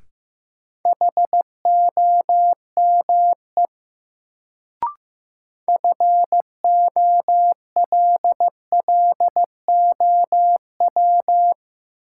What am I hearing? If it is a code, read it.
Morse code, heard as a keyed tone, about 16 wpm, 700 Hz. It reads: HOME FOLLOW
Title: Sentences from Top 200 Words - New Word Drill - 15wpm